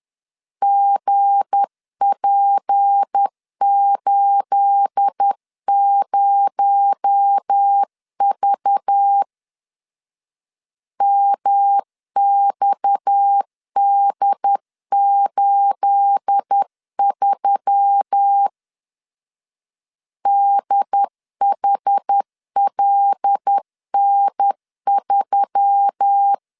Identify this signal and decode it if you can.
Morse code, heard as a keyed tone, about 10 wpm, 800 Hz. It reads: GP80V MXD83 DHLN3
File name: LEZIONI / 9 28 10wpm 2842